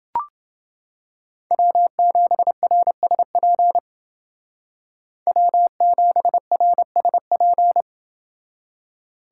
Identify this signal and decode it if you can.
Morse code, heard as a keyed tone, about 29 wpm, 700 Hz. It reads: W7RSP W7RSP